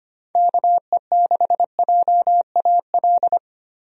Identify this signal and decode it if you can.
Morse code, heard as a keyed tone, about 25 wpm, 700 Hz. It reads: KE6JAL
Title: USA Call Signs - 6 Characters - Repeated 2x - 25wpm